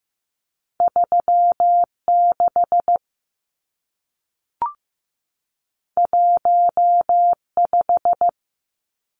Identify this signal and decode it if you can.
Morse code, heard as a keyed tone, about 15 wpm, 700 Hz. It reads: 36 15
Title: Numbers - 2-Digit 15wpm